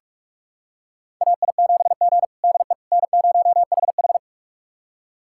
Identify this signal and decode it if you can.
Morse code, heard as a keyed tone, about 45 wpm, 700 Hz. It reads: AI7G DE N0HH
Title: USA Callsign DE Callsign - 4 Characters - 45wpm